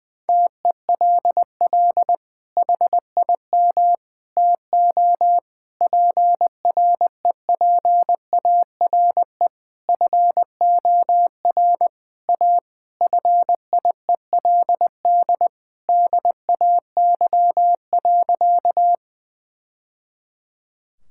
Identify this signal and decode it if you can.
Morse code, heard as a keyed tone, about 20 wpm, 700 Hz. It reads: TELL HIM TO PREPARE FOR A FIELD DAY.
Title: Sentences from Top 1000 Words - Review Part 6 - 20wpm